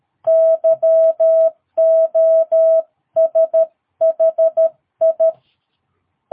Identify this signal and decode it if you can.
Morse code, heard as a keyed tone, about 13 wpm, 650 Hz. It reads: YOSHI
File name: jh8cbh / 2022cw